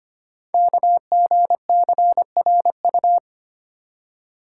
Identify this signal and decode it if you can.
Morse code, heard as a keyed tone, about 25 wpm, 700 Hz. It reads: KGCRU
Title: Code Groups - 5 Letters - 25wpm